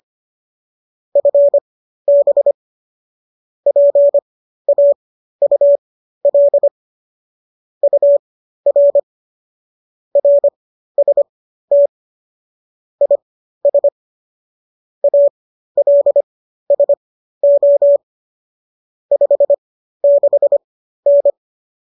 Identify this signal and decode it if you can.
Morse code, heard as a keyed tone, about 25 wpm, spaced as for 13 wpm, 570 Hz. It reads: FB PAUL UR RST IS ALSO 56N